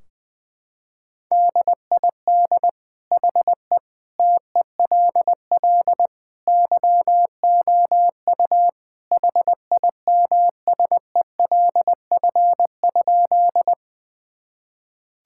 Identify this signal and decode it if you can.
Morse code, heard as a keyed tone, about 21 wpm, 700 Hz. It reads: DID HE TELL YOU HIMSELF?
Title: Sentences from Top 1600 Words 20wpm